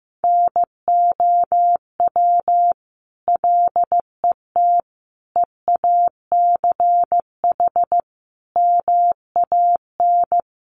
Morse code, heard as a keyed tone, about 15 wpm, 700 Hz. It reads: NOW LET EACH MAN
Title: Sentences from Top 1500 Words 15wpm